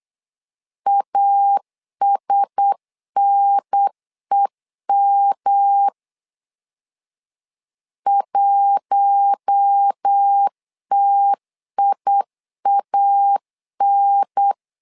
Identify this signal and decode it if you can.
Morse code, heard as a keyed tone, about 8 wpm, 800 Hz. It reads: ASNEM 1TIAN
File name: LEZIONI / 4 Q1S 8wpm 2710